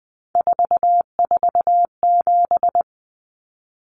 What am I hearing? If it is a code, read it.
Morse code, heard as a keyed tone, about 20 wpm, 700 Hz. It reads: 447
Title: Common Cut Numbers - 3-Digit 20wpm